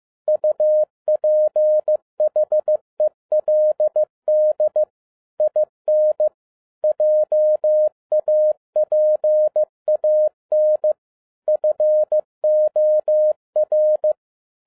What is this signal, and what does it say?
Morse code, heard as a keyed tone, about 15 wpm, 610 Hz. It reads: UPHELD IN JAPAN FOR
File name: 2011-07-mp3 / n37 THE 3.11 2 15wpm